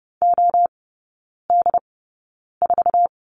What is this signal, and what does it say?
Morse code, heard as a keyed tone, about 30 wpm, spaced as for 9 wpm, 700 Hz. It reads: OD4
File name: course / 30:10wpm / 087 Lesson 15e - Number 5 - Three Characters 30wpm